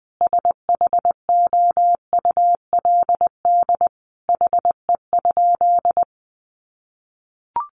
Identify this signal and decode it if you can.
Morse code, heard as a keyed tone, about 20 wpm, 700 Hz. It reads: SHOULD HE?
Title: Sentences from Top 300 Words 20wpm